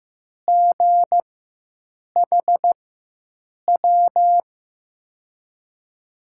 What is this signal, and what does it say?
Morse code, heard as a keyed tone, about 15 wpm, spaced as for 7 wpm, 700 Hz. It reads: GHW
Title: USA Call Signs - 4x Character Spacing - 15wpm